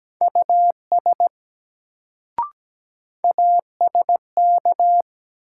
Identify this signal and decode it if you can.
Morse code, heard as a keyed tone, about 17 wpm, 700 Hz. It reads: US ASK
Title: Sentences from Top 200 Words - New Word Drill - 17wpm